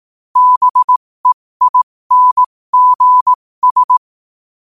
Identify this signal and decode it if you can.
Morse code, heard as a keyed tone, about 18 wpm, spaced as for 15 wpm, 1000 Hz. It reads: BEINGS
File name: qotdfiles / 15wpm